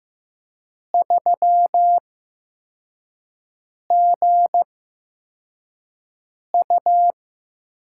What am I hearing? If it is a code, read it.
Morse code, heard as a keyed tone, about 15 wpm, spaced as for 4 wpm, 700 Hz. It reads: 3GU
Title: USA Call Signs - 4 Characters - 8x Character Spacing - 15wpm